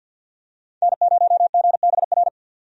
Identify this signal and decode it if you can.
Morse code, heard as a keyed tone, about 50 wpm, 700 Hz. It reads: N0GBR
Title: USA Call Signs - 5 Characters - 50wpm